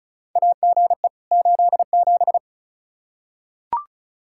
Morse code, heard as a keyed tone, about 36 wpm, 700 Hz. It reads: AGE 87
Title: K1USN SST QSO Practice 35wpm